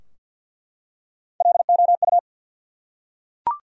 Morse code, heard as a keyed tone, about 50 wpm, 700 Hz. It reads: LOU
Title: USA Common Names - 50wpm